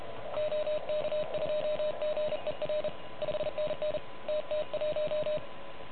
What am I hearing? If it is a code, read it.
Morse code, heard as a keyed tone, about 32 wpm, 600 Hz, with noise. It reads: OK2GER 5NN TT1
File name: truSDX / MWC29.8